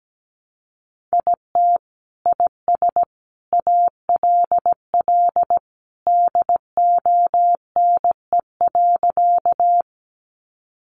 Morse code, heard as a keyed tone, about 17 wpm, 700 Hz. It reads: IT IS ALL DONE.
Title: Sentences from Top 500 Words 17wpm